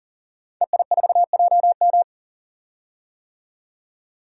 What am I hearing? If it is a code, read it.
Morse code, heard as a keyed tone, about 40 wpm, 700 Hz. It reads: EI4JM
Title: DXCC-100 Call Signs - 5 Characters - 40wpm